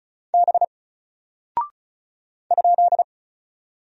Morse code, heard as a keyed tone, about 34 wpm, 700 Hz. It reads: B ?
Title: Single Character (Letter:Number:Punctuation:Prosign) Alt 35wpm